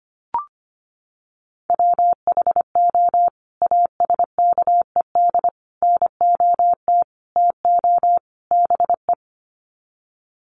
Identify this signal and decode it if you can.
Morse code, heard as a keyed tone, about 25 wpm, 700 Hz. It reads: WHO ASKED NOT TO BE
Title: Sets of 5 Words Encore2 25wpm